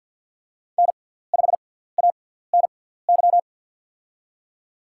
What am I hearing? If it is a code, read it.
Morse code, heard as a keyed tone, about 53 wpm, spaced as for 17 wpm, 700 Hz. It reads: N5ANY
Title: USA Call Signs - 6x Character Spacing - 50wpm